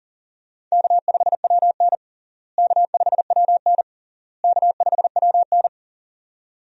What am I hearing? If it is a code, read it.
Morse code, heard as a keyed tone, about 40 wpm, 700 Hz. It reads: K5WN K5WN K5WN
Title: USA Call Signs - 4 Characters - Repeated 3x - 40wpm